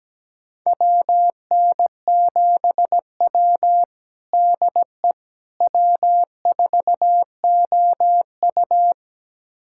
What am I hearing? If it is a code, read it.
Morse code, heard as a keyed tone, about 17 wpm, 700 Hz. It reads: WN7W DE W4OU
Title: USA Callsign DE Callsign - 4 Characters - 17wpm